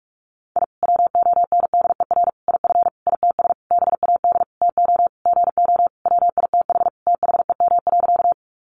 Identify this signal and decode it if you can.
Morse code, heard as a keyed tone, about 44 wpm, 700 Hz. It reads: I WONDER IF ITS BAD TO GO WITH THEM.
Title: Sentences from Top 700 Words 45wpm